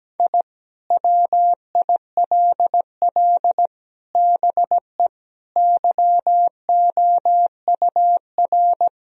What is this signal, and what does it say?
Morse code, heard as a keyed tone, about 17 wpm, 700 Hz. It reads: I WILL BE YOUR
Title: Sentences from Top 1000 Words - Review Part 4 - 17wpm